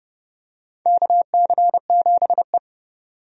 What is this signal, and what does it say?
Morse code, heard as a keyed tone, about 32 wpm, 700 Hz. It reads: KC7E
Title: USA Call Signs - 4 Characters - 30wpm